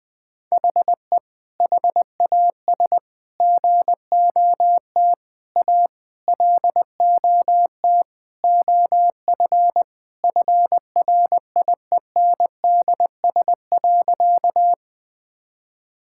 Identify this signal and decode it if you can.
Morse code, heard as a keyed tone, about 20 wpm, 700 Hz. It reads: HE HAS GOT A LOT OF FRIENDS.
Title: Sentences with 4-Word Phrases - Spoken Beginning - 20wpm